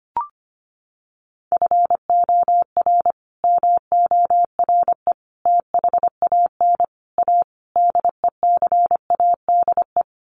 Morse code, heard as a keyed tone, about 25 wpm, 700 Hz. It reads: FOR MORE THAN A DECADE
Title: Sets of 5 Words 25wpm